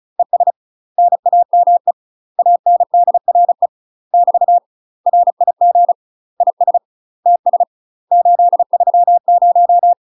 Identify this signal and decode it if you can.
Morse code, heard as a keyed tone, about 34 wpm, 700 Hz. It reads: ES NAME ANDRE <BT> RIG IS TS 830
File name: UK MASTER QSOs TRAINING / 35WPM / UK MASTER QSOs 0021